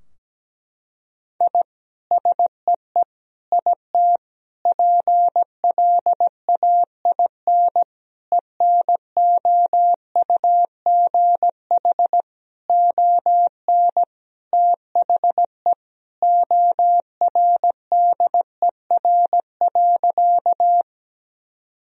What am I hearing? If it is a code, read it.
Morse code, heard as a keyed tone, about 17 wpm, 700 Hz. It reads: I SEE IT PLAIN ENOUGH ON THE ORDER.
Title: Sentences from Top 1000 Words - Review Part 2 - 17wpm